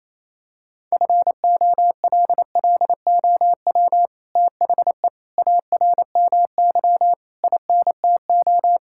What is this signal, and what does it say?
Morse code, heard as a keyed tone, about 28 wpm, 700 Hz. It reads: FOLLOW THE ARMY INTO
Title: Sentences from Top 1000 Words - Review Part 5 - 28wpm